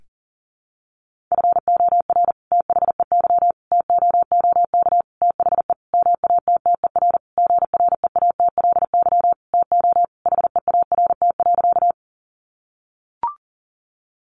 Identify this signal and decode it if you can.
Morse code, heard as a keyed tone, about 40 wpm, 700 Hz. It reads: FOR THEY TOOK THE MATTER GREATLY TO HEART.
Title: Sentences from Top 1400 Words 40wpm